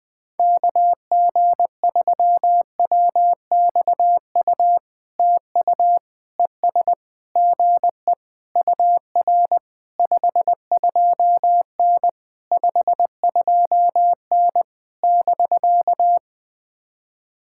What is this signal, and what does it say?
Morse code, heard as a keyed tone, about 19 wpm, 700 Hz. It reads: KG3WXU TU ES GE UR 52N 52N <BK>
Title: POTA QSO Practice - 20wpm